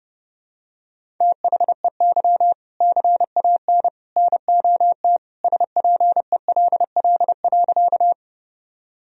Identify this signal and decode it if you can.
Morse code, heard as a keyed tone, about 30 wpm, 700 Hz. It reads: THEY CAN NOT SPELL.
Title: Sentences from Top 1000 Words - Review Part 1 - 30wpm